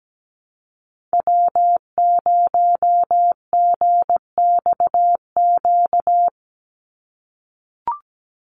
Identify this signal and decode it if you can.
Morse code, heard as a keyed tone, about 17 wpm, 700 Hz. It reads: W0GXQ
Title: Call Signs - 17wpm